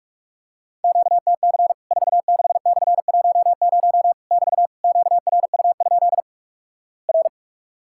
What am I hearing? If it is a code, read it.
Morse code, heard as a keyed tone, about 46 wpm, 700 Hz. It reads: QTC 46/10 <BT> QRU? R
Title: Worked All Europe DX Contest - QTC Exchange - 45wpm